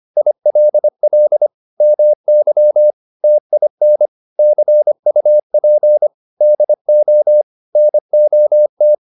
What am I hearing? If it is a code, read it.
Morse code, heard as a keyed tone, about 25 wpm, 600 Hz. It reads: ILL MY TIN CUP DO NOT